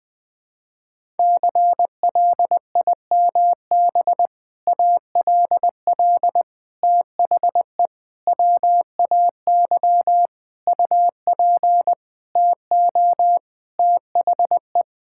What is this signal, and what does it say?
Morse code, heard as a keyed tone, about 20 wpm, 700 Hz. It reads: CLIMB ALL THE WAY UP TO THE
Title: Sentences with 6-Word Phrases - 20wpm